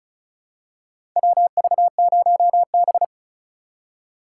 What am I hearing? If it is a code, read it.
Morse code, heard as a keyed tone, about 36 wpm, 700 Hz. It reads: WV0B
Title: USA Call Signs - 4 Characters - 8x Character Spacing - 35wpm